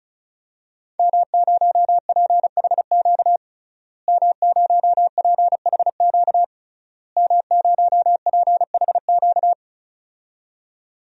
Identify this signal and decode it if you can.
Morse code, heard as a keyed tone, about 34 wpm, 700 Hz. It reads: M0PHQ M0PHQ M0PHQ
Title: DXCC-100 Call Signs - 5 Characters - Repeated 3x - 35wpm